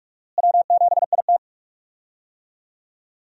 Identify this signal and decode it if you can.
Morse code, heard as a keyed tone, about 43 wpm, 700 Hz. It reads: W7IT
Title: USA Call Signs - 4 Characters - Repeated 3x - 45wpm